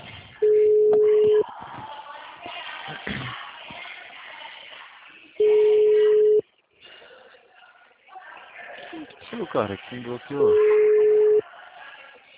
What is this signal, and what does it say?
Ringback tone (Europe, 425 Hz), with speech.